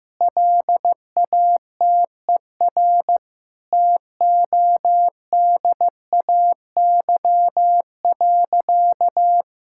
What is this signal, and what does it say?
Morse code, heard as a keyed tone, about 15 wpm, 700 Hz. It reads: LATER TODAY.
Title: Sentences from Top 1000 Words - Review Part 5 - 15wpm